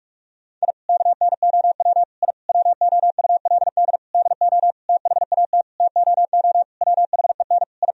Morse code, heard as a keyed tone, about 46 wpm, 700 Hz. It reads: I KNOW I WOULD DO THAT TOO WHEN I
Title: Sentences from Top 1000 Words - Review Part 3 - 45wpm